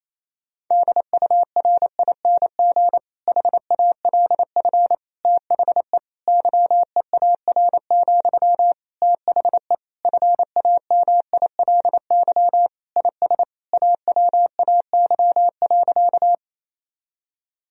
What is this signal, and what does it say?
Morse code, heard as a keyed tone, about 28 wpm, 700 Hz. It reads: DURING HALF THE YEAR, THE FAMILY IS AWAY.
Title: Sentences from Top 1000 Words - Review Part 2 - 28wpm